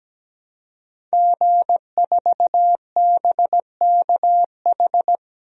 Morse code, heard as a keyed tone, about 17 wpm, 700 Hz. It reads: G4BKH
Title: DXCC-100 Call Signs - 5 Characters - Repeated 3x - 17wpm